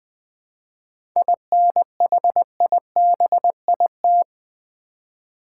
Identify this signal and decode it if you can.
Morse code, heard as a keyed tone, about 20 wpm, 690 Hz. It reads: INHIBIT
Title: 7 Letter Words - First 3 Letters Given - 20wpm